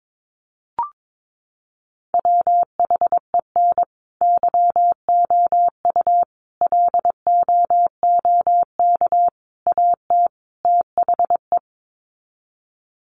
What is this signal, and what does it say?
Morse code, heard as a keyed tone, about 22 wpm, 700 Hz. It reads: WHEN YOU LOOK AT THE
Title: Sets of 5 Words 22wpm